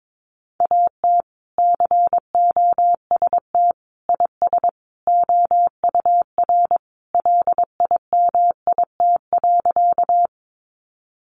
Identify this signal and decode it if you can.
Morse code, heard as a keyed tone, about 22 wpm, 700 Hz. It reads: AT COST IS OUR LIMIT.
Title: Sentences from Top 1500 Words 22wpm